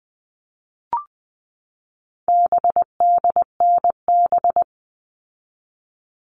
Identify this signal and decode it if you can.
Morse code, heard as a keyed tone, about 20 wpm, 700 Hz. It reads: BDNB
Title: All Cut Numbers - 4-Digit 20wpm